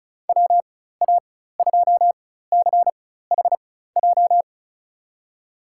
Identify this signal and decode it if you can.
Morse code, heard as a keyed tone, about 36 wpm, spaced as for 16 wpm, 700 Hz. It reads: WA2CHJ
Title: USA Call Signs - 6 Characters - 4x Character Spacing - 35wpm